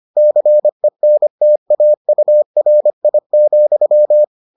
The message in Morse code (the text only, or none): CENTAURI,